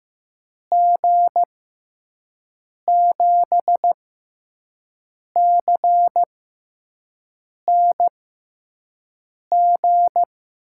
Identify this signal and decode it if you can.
Morse code, heard as a keyed tone, about 15 wpm, spaced as for 5 wpm, 700 Hz. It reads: G7CNG
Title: USA Call Signs - 6 Characters - 6x Character Spacing - 15wpm